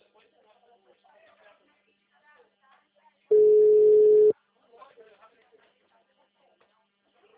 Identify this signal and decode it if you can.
Ringback tone (Europe, 425 Hz), with speech.